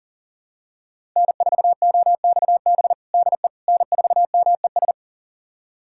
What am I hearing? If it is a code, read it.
Morse code, heard as a keyed tone, about 40 wpm, 700 Hz. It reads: N4OXB DE N4MES
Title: USA Callsign DE Callsign - 5 Characters - 40wpm